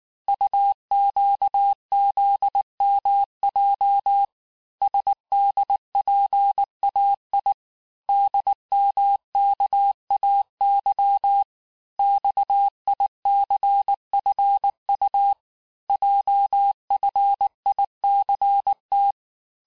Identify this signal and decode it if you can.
Morse code, heard as a keyed tone, about 19 wpm, 800 Hz. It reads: UQZMJ SDPAI DMKAY XICFU JFICT
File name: cours-du-mercredi / audios / cours-869-texte-CHIFFRE-vitesse-18-MPM